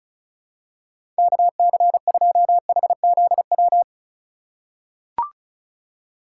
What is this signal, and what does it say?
Morse code, heard as a keyed tone, about 34 wpm, 700 Hz. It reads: KC2HZW
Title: USA Call Signs - 6 Characters - 2x Character Spacing - 35wpm